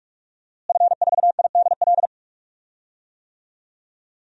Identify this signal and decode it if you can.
Morse code, heard as a keyed tone, about 46 wpm, 700 Hz. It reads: F4IDL
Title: DXCC-100 Call Signs - 5 Characters - Repeated 2x - 45wpm